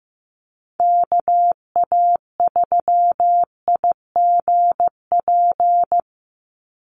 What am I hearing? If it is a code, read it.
Morse code, heard as a keyed tone, about 15 wpm, 700 Hz. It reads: KA3IGP
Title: USA Call Signs 15wpm